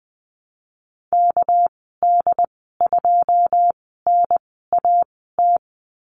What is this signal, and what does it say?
Morse code, heard as a keyed tone, about 20 wpm, spaced as for 14 wpm, 700 Hz. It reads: KD2NAT
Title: 245 Lesson 39g - Letter z - Call Signs 20wpm